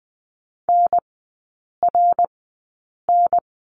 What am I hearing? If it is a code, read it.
Morse code, heard as a keyed tone, about 20 wpm, spaced as for 8 wpm, 700 Hz. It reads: NRN
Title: 053 Lesson 10e - Letter r - Three Characters 20wpm